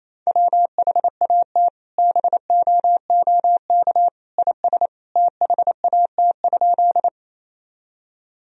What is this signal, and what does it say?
Morse code, heard as a keyed tone, about 28 wpm, 700 Hz. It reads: WHAT BOOK IS THAT?
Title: Sentences from Top 1000 Words - Review Part 1 - 28wpm